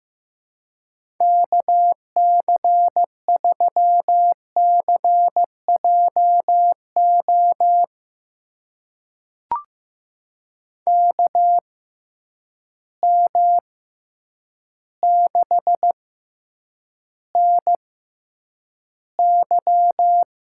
Morse code, heard as a keyed tone, about 15 wpm, spaced as for 8 wpm, 700 Hz. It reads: KC3CJO KM6NY
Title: USA Call Signs - 6 Characters - 6x Character Spacing - 15wpm